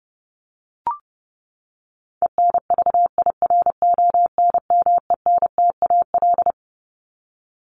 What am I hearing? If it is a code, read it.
Morse code, heard as a keyed tone, about 30 wpm, 700 Hz. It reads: ENVIRONMENTAL
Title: Top 1001-1250 Words 30wpm